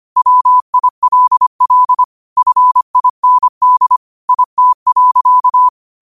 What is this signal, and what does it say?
Morse code, heard as a keyed tone, about 25 wpm, 1000 Hz. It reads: WILL FIND IT.
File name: qotdfiles / 25wpm